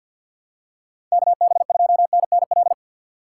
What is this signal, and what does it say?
Morse code, heard as a keyed tone, about 53 wpm, 700 Hz. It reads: KB2NNL
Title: USA Call Signs - 6x Character Spacing - 50wpm